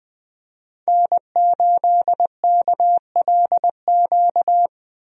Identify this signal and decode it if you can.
Morse code, heard as a keyed tone, about 20 wpm, 700 Hz. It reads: N8KLQ
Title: USA Call Signs - 5 Characters - Repeated 3x - 20wpm